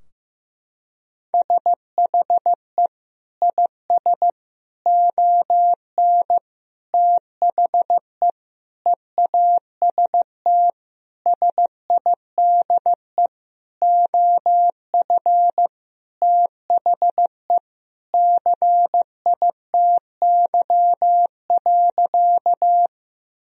Morse code, heard as a keyed tone, about 15 wpm, 700 Hz. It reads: SHE IS ON THE EAST SIDE OF THE CITY.
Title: Sentences from Top 1000 Words - Review Part 3 - 15wpm